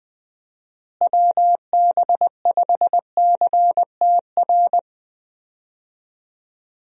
Morse code, heard as a keyed tone, about 20 wpm, 700 Hz. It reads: WB5CTR
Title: USA Call Signs - 6 Characters - Repeated 2x - 20wpm